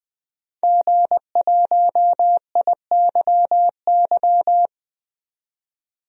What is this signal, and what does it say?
Morse code, heard as a keyed tone, about 20 wpm, 700 Hz. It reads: G1IYY
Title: DXCC-100 Call Signs - 5 Characters - 20wpm